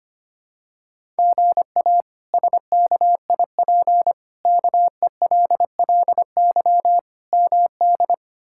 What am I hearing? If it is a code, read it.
Morse code, heard as a keyed tone, about 25 wpm, 700 Hz. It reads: GA SKIP KELLY MD
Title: K1USN SST QSO Practice - 25wpm